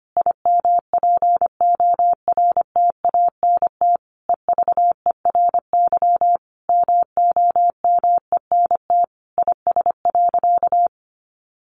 Morse code, heard as a keyed tone, about 25 wpm, 700 Hz. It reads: IMPORTANT EVERY MOMENT IS.
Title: Sentences from Top 1000 Words 25wpm